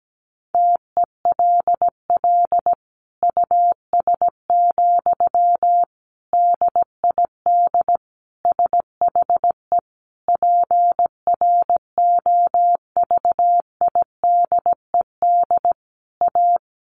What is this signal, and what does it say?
Morse code, heard as a keyed tone, about 17 wpm, 700 Hz. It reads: TELL US, DID SHE PROVIDED A
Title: Sentences from Top 1500 Words 17wpm